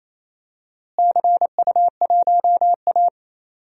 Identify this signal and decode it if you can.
Morse code, heard as a keyed tone, about 28 wpm, 700 Hz. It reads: CU1A
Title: DXCC-100 Call Signs - 4 Characters - Repeated 3x - 28wpm